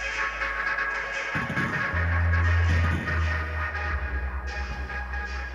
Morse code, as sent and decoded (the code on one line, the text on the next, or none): none